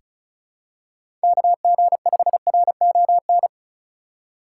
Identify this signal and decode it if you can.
Morse code, heard as a keyed tone, about 36 wpm, 700 Hz. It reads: KG5RON
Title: USA Call Signs - 6 Characters - 2x Character Spacing - 35wpm